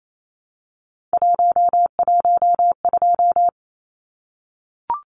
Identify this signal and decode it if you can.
Morse code, heard as a keyed tone, about 28 wpm, 700 Hz. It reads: 112
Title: Numbers - 3-Digit 28wpm